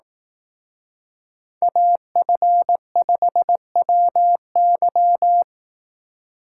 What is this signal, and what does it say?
Morse code, heard as a keyed tone, about 18 wpm, 700 Hz. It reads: AF5WY
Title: Speed Racing - USA 2x2 Call Signs 15wpm